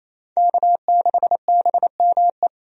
Morse code, heard as a keyed tone, about 27 wpm, 700 Hz. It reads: K6BME
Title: USA Call Signs - 5 Characters - 28wpm